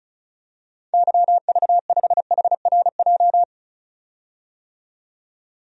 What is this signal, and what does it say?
Morse code, heard as a keyed tone, about 36 wpm, 700 Hz. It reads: YV5HRJ
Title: DXCC-100 Call Signs - 6 Characters - 35wpm